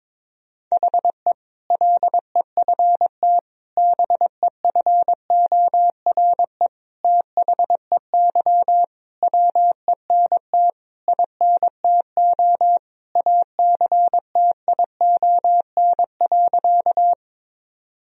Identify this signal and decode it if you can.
Morse code, heard as a keyed tone, about 22 wpm, 700 Hz. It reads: HE LEFT BEFORE THEY WENT INTO ACTION.
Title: Sentences from Top 1000 Words - Review Part 4 - 22wpm